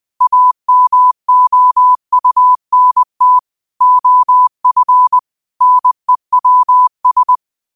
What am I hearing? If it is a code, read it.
Morse code, heard as a keyed tone, about 20 wpm, 1000 Hz. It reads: AMOUNT OF NEWS
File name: qotdfiles / 20wpm